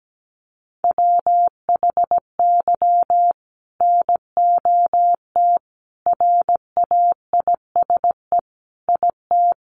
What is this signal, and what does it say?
Morse code, heard as a keyed tone, about 17 wpm, 700 Hz. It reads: WHY NOT RAISE IT
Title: Sentences from Top 1400 Words 17wpm